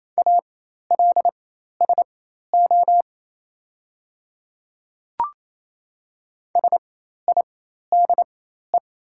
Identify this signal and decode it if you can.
Morse code, heard as a keyed tone, about 27 wpm, spaced as for 13 wpm, 690 Hz. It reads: ALSO SIDE
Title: 4x Character Spacing - Top 200 Words - 28wpm